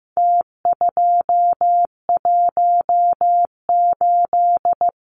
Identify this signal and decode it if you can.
Morse code, heard as a keyed tone, about 15 wpm, 700 Hz. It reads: T218
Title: Numbers - Contest Style - 4-Digit 15wpm